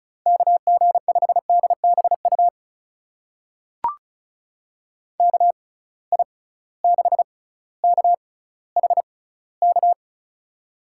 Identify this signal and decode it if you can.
Morse code, heard as a keyed tone, about 36 wpm, spaced as for 18 wpm, 700 Hz. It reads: KG5DBU KI6KHK
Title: USA Call Signs - 6 Characters - 6x Character Spacing - 35wpm